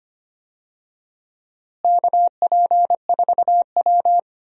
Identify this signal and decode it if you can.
Morse code, heard as a keyed tone, about 25 wpm, 700 Hz. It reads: KP4W
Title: DXCC-100 Call Signs - 4 Characters - Repeated 2x - 25wpm